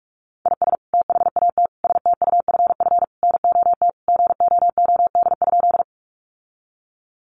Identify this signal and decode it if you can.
Morse code, heard as a keyed tone, about 44 wpm, 700 Hz. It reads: IS THAT STUFF NOT GOOD?
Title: Sentences from Top 1400 Words 45wpm